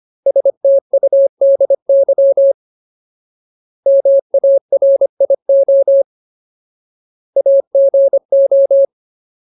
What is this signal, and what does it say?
Morse code, heard as a keyed tone, about 25 wpm, 550 Hz. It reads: STUDY MARIO AGO
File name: Ultimate Word List / 25WPM Ultimate